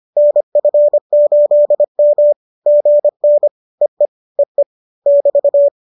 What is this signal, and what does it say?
Morse code, heard as a keyed tone, about 25 wpm, 590 Hz. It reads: NF8M GN EE EE <BT>